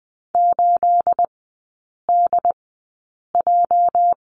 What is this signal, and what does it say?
Morse code, heard as a keyed tone, about 20 wpm, spaced as for 8 wpm, 700 Hz. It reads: 8DJ
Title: 218 Lesson 35e - Number 0 - Three Characters 20wpm